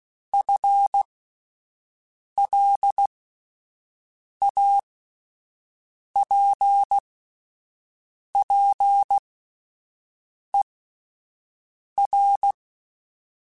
Morse code, heard as a keyed tone, about 16 wpm, spaced as for 5 wpm, 800 Hz. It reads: FLAPPER
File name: Spaced / FL Runs sp mp3s / FL Runs sp @10